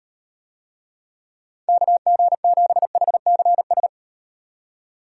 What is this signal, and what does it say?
Morse code, heard as a keyed tone, about 38 wpm, 700 Hz. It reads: KG7HCS